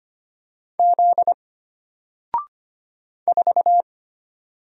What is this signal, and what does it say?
Morse code, heard as a keyed tone, about 25 wpm, 700 Hz. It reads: Z 4
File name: Single-Letter-Number-25wpm